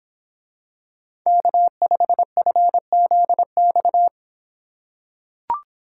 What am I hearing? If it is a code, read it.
Morse code, heard as a keyed tone, about 24 wpm, 700 Hz. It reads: K5FZX